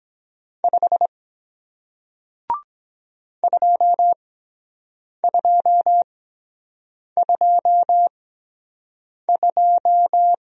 Morse code, heard as a keyed tone, about 23 wpm, spaced as for 7 wpm, 700 Hz. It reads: 5 2222